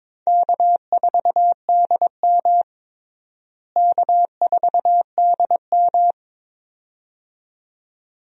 Morse code, heard as a keyed tone, about 22 wpm, 700 Hz. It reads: K4DM K4DM
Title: USA Call Signs - 4 Characters - Repeated 2x - 22wpm